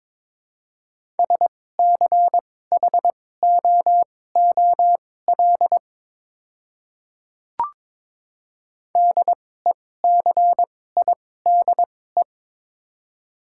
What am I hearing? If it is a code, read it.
Morse code, heard as a keyed tone, about 21 wpm, spaced as for 16 wpm, 700 Hz. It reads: SCHOOL DECIDE
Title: 2x Character Spacing - Top 500 Words - 22wpm